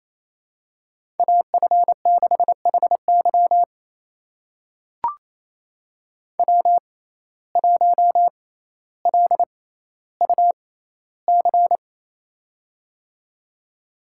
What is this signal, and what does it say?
Morse code, heard as a keyed tone, about 28 wpm, spaced as for 14 wpm, 700 Hz. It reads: AF6HY W1LUC
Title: USA Call Signs - 5 Characters - 6x Character Spacing - 28wpm